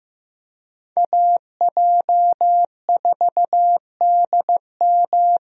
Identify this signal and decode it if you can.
Morse code, heard as a keyed tone, about 15 wpm, 700 Hz. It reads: AJ4DM
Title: USA Call Signs - 5 Characters - Repeated 2x - 15wpm